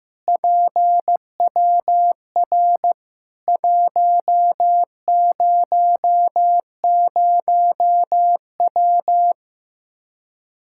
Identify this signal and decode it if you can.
Morse code, heard as a keyed tone, about 15 wpm, 700 Hz. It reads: PWR 100W
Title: K1USN SST QSO Practice 15wpm